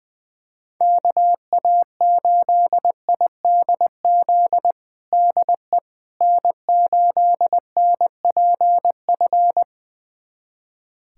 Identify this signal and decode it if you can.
Morse code, heard as a keyed tone, about 20 wpm, 700 Hz. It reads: KA8IDZ DE N8NPF
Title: USA Callsign DE Callsign 20wpm